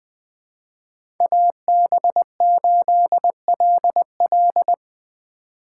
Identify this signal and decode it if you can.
Morse code, heard as a keyed tone, about 20 wpm, 700 Hz. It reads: AB8LL
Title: 2023 POTA Hunter Call Signs - 20wpm